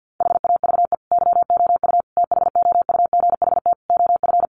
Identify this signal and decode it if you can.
Morse code, heard as a keyed tone, about 49 wpm, 700 Hz. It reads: HAVE YOU THOUGHT OF
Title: Sentences from Top 700 Words 50wpm